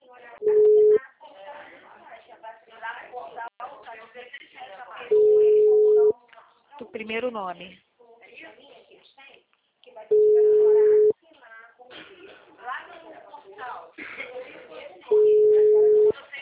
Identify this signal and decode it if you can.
Ringback tone (Europe, 425 Hz), with speech.